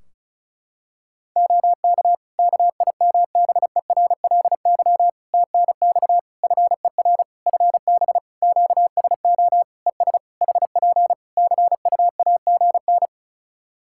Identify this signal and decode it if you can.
Morse code, heard as a keyed tone, about 36 wpm, 700 Hz. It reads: OK KIMBERLY TNX FER FB QSO ES HP CUAGN
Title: Practice QSO - Exchange 4 - Ending - 35wpm